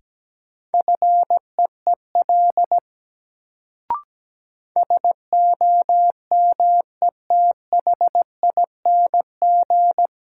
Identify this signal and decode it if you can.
Morse code, heard as a keyed tone, about 17 wpm, 700 Hz. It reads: FEEL SOMETHING
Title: Sets of 5 Words - New Word Drill - 17wpm